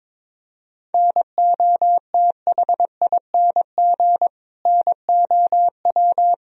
Morse code, heard as a keyed tone, about 22 wpm, 700 Hz. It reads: NOTHING NOW
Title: Sentences from Top 1000 Words - Review Part 1 - 22wpm